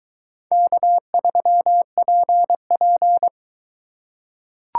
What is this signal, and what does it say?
Morse code, heard as a keyed tone, about 21 wpm, 700 Hz. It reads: K3PP